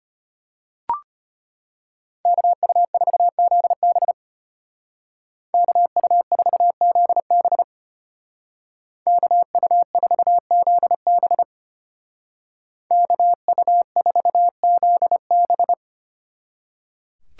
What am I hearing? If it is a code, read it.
Morse code, heard as a keyed tone, about 33 wpm, 700 Hz. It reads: KU4ZB KU4ZB KU4ZB KU4ZB